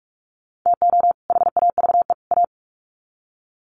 Morse code, heard as a keyed tone, about 44 wpm, 700 Hz. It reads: TO HAVE A
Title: Sets of 3 Words 45wpm